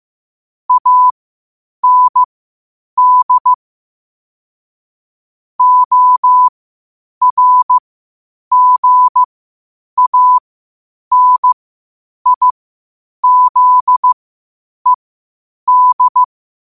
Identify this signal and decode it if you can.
Morse code, heard as a keyed tone, about 15 wpm, spaced as for 8 wpm, 1000 Hz. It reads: AND ORGANIZED
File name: qotdfiles / 8wpm